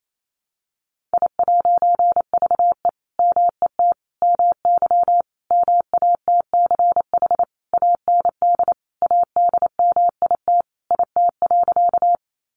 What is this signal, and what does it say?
Morse code, heard as a keyed tone, about 28 wpm, 700 Hz. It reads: I'VE MET MY MATCH AND ADMIT IT.
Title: A Sound Of Thunder 28wpm